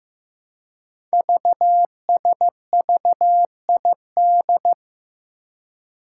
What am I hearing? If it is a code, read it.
Morse code, heard as a keyed tone, about 15 wpm, 700 Hz. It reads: VSVID
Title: Code Groups - 5 Letters - 15wpm